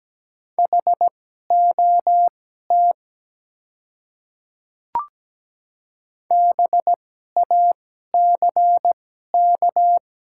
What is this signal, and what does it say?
Morse code, heard as a keyed tone, about 17 wpm, spaced as for 12 wpm, 700 Hz. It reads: HOT BACK
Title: 2x Character Spacing - Top 300 Words - 17wpm